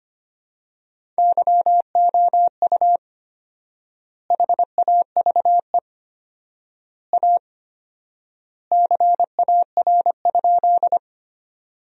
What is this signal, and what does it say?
Morse code, heard as a keyed tone, about 25 wpm, 700 Hz. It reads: YOU HAVE A CAR?
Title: 4x Word-Spacing - Sentences from Top 300 Words - 25wpm